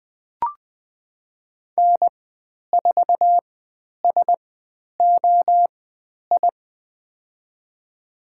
Morse code, heard as a keyed tone, about 20 wpm, spaced as for 10 wpm, 700 Hz. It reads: N4SOI
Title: 109 Lesson 18g - Letter m - Call Signs 20wpm